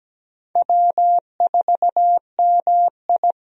Morse code, heard as a keyed tone, about 17 wpm, 700 Hz. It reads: W4MI
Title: USA Call Signs - 4 Characters - 17wpm